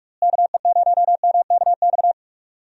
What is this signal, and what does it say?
Morse code, heard as a keyed tone, about 44 wpm, 700 Hz. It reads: KE0MKX
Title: USA Call Signs - 6 Characters - Repeated 3x - 45wpm